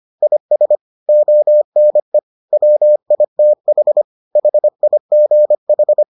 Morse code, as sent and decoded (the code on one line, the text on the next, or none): .. ... / --- -. / .-- .. - .... / .... .. --. ....
IS ONE WITH HIGH